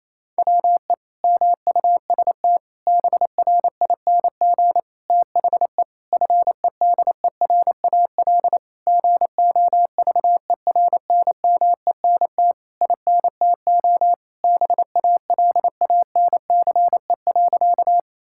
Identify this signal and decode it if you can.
Morse code, heard as a keyed tone, about 27 wpm, 700 Hz. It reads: WE MUST BRING THE FEDERAL GOVERNMENT INTO BALANCE.
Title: Sentences from Top 1000 Words - Review Part 6 - 28wpm